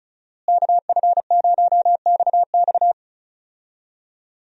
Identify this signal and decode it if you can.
Morse code, heard as a keyed tone, about 34 wpm, 700 Hz. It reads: KF0XX
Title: USA Call Signs - 5 Characters - Repeated 2x - 35wpm